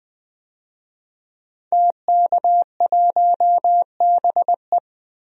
Morse code, heard as a keyed tone, about 20 wpm, 700 Hz. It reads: TK1BE
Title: DXCC-100 Call Signs - 5 Characters - 20wpm